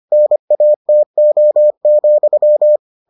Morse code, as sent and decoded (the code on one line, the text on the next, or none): -. .- - --- --..--
NATO,